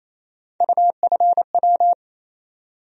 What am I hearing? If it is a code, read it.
Morse code, heard as a keyed tone, about 28 wpm, 700 Hz. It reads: UFW
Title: USA Call Sign Postfixes - 3 Letters 28wpm